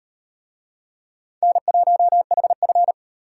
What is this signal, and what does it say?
Morse code, heard as a keyed tone, about 33 wpm, 700 Hz. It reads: N1HF